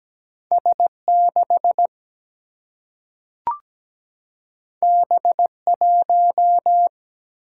Morse code, heard as a keyed tone, about 17 wpm, 700 Hz. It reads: S6 B1
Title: Single-Letter Single-Number 17wpm